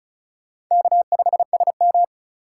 Morse code, heard as a keyed tone, about 44 wpm, 700 Hz. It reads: K5SM